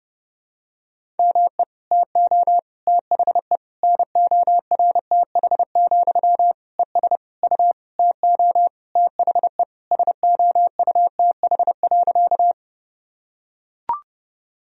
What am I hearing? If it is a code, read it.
Morse code, heard as a keyed tone, about 30 wpm, 700 Hz. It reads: ME TO THE NORTH, ES U TO THE SOUTH.
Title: Sentences from Top 400 Words and Abbreviations 30wpm